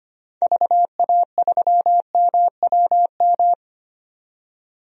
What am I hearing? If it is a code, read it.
Morse code, heard as a keyed tone, about 25 wpm, 700 Hz. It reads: VA3MWM
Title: DXCC-100 Call Signs - 6 Characters - 25wpm